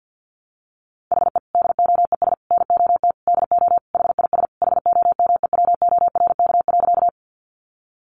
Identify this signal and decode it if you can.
Morse code, heard as a keyed tone, about 48 wpm, 700 Hz. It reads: HE DOES NOT DO HIS HOMEWORK.
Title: Additional Easy Phrases 50wpm